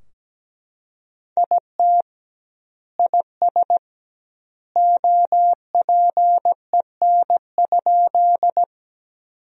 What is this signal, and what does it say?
Morse code, heard as a keyed tone, about 17 wpm, 700 Hz. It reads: IT IS OPEN?
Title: 2x Word-Spacing - Sentences from Top 300 Words - 17wpm